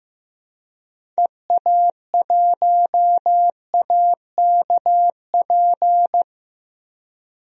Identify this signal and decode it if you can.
Morse code, heard as a keyed tone, about 15 wpm, 700 Hz. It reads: EA1AKP
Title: DXCC-100 Call Signs - 6 Characters - Repeated 3x - 15wpm